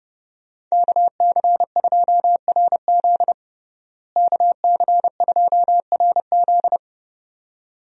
Morse code, heard as a keyed tone, about 30 wpm, 700 Hz. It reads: KC2RZ KC2RZ
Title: USA Call Signs - 5 Characters - Repeated 2x - 30wpm